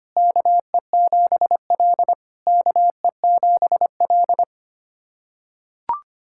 Morse code, heard as a keyed tone, about 25 wpm, 700 Hz. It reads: KE7L KE7L
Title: SOTA QSO Practice 25wpm